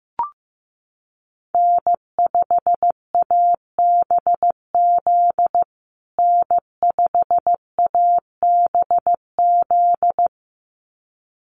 Morse code, heard as a keyed tone, about 15 wpm, 700 Hz. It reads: N5ABZ N5ABZ
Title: USA Call Signs Repeated 2x 15wpm